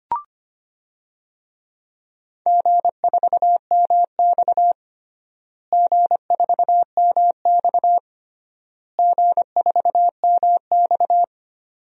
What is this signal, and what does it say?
Morse code, heard as a keyed tone, about 25 wpm, 700 Hz. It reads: G4MX G4MX G4MX
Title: DXCC-100 Call Signs - 4 Characters - Repeated 3x - 25wpm